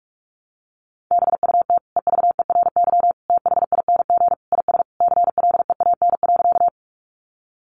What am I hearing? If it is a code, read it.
Morse code, heard as a keyed tone, about 44 wpm, 700 Hz. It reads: BUT EVERYTHING IS CLEAN.
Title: Sentences from Top 1100 Words 45wpm